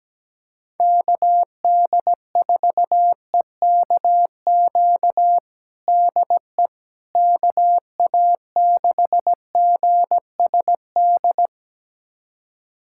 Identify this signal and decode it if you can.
Morse code, heard as a keyed tone, about 17 wpm, 700 Hz. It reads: KD4EKQ DE KA6GSD
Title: USA Callsign DE Callsign - 6 Characters - 17wpm